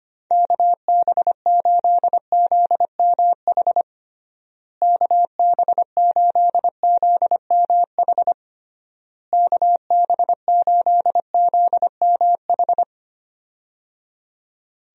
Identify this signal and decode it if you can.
Morse code, heard as a keyed tone, about 25 wpm, 700 Hz. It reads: KB8ZMH KB8ZMH KB8ZMH
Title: USA Call Signs - 6 Characters - Repeated 3x - 25wpm